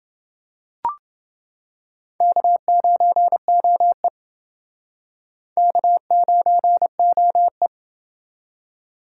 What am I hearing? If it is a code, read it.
Morse code, heard as a keyed tone, about 28 wpm, 700 Hz. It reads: K9OE K9OE